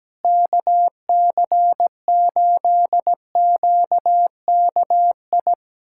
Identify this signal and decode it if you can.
Morse code, heard as a keyed tone, about 17 wpm, 700 Hz. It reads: KC8QKI
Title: USA Callsign DE Callsign 17wpm